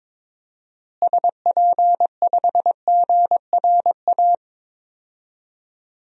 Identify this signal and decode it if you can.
Morse code, heard as a keyed tone, about 22 wpm, 700 Hz. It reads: SP5GRA
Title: DXCC-100 Call Signs - 6 Characters - Repeated 2x - 22wpm